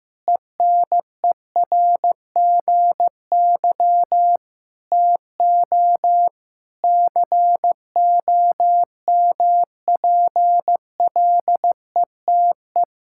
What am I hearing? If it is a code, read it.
Morse code, heard as a keyed tone, about 14 wpm, 700 Hz. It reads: ENERGY TO COMPLETE
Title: Sentences from Top 1000 Words - Review Part 6 - 15wpm